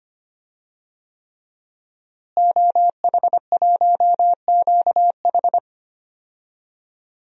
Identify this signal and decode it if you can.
Morse code, heard as a keyed tone, about 25 wpm, 700 Hz. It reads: OH1QH
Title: DXCC-100 Call Signs - 5 Characters - Repeated 3x - 25wpm